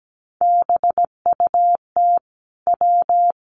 Morse code, heard as a keyed tone, about 17 wpm, 700 Hz. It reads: BUT W
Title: Sets of 2 Words 17wpm